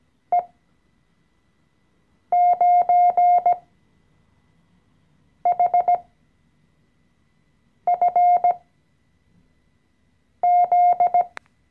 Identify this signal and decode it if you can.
Morse code, heard as a keyed tone, about 17 wpm, spaced as for 4 wpm, 710 Hz. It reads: E9HFZ
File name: CW-Online-Kurs / Anfänger / Lektion 014